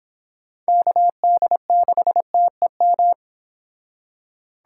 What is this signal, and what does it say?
Morse code, heard as a keyed tone, about 26 wpm, 700 Hz. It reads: KD6TEM